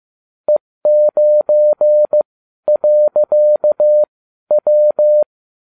Morse code, heard as a keyed tone, about 15 wpm, spaced as for 12 wpm, 600 Hz. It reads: E9.W